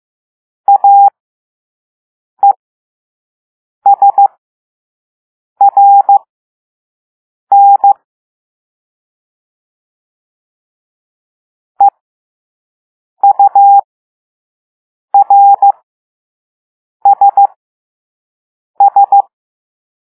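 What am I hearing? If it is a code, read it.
Morse code, heard as a keyed tone, about 15 wpm, spaced as for 5 wpm, 800 Hz. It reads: AESRN EURSS
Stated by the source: Morse code lesson 5 drill of random five-letter groups using all letters learned so far.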